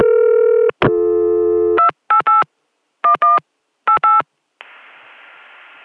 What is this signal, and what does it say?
DTMF, heard as telephone touch tones, tones of about 120 ms. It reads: 288 11 88